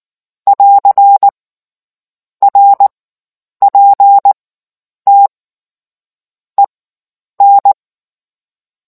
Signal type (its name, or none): Morse code